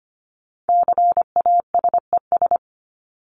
Morse code, heard as a keyed tone, about 25 wpm, 700 Hz. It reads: CASES
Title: ES:S Suffix - Top 100 Words with Base Words 25wpm